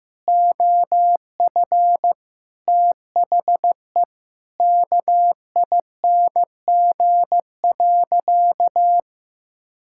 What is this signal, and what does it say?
Morse code, heard as a keyed tone, about 15 wpm, 700 Hz. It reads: OF THE KING.
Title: Sentences from Top 1000 Words - Review Part 5 - 15wpm